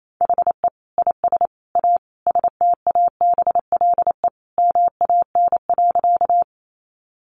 Morse code, heard as a keyed tone, about 28 wpm, 700 Hz. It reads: HE IS A STABLE MAN.
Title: Sentences from Top 1500 Words 28wpm